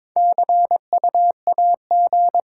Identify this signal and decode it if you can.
Morse code, heard as a keyed tone, about 22 wpm, 700 Hz. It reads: C U A G
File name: Top 20 QSO Elements - 6x Character Spacing - 22wpm